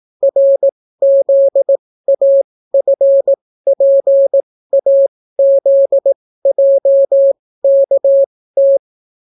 Morse code, heard as a keyed tone, about 18 wpm, spaced as for 14 wpm, 550 Hz. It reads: RZAFPAZJKT